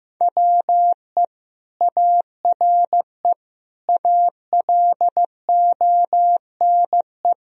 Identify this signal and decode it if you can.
Morse code, heard as a keyed tone, about 15 wpm, 700 Hz. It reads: WE ARE ALONE
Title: Sentences from Top 1000 Words - Review Part 1 - 15wpm